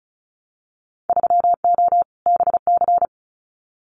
Morse code, heard as a keyed tone, about 36 wpm, 700 Hz. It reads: 3O BC
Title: Winter Field Day Exchanges Encore 35wpm